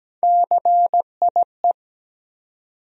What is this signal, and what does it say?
Morse code, heard as a keyed tone, about 17 wpm, 700 Hz. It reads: CIE
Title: USA Call Sign Postfixes - 3 Letters 17wpm